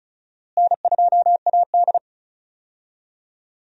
Morse code, heard as a keyed tone, about 36 wpm, 700 Hz. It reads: N2AD
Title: USA Call Signs - 4 Characters - Repeated 3x - 35wpm